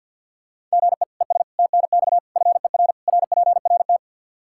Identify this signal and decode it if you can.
Morse code, heard as a keyed tone, about 51 wpm, 700 Hz. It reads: GE ES TNX FER RPRT
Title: K1USN SST QSO Practice 50wpm